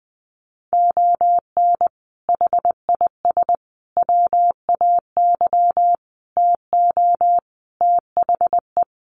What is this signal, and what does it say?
Morse code, heard as a keyed tone, about 20 wpm, 700 Hz. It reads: ON HIS WAY TO THE
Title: Sets of 5 Words Encore2 20wpm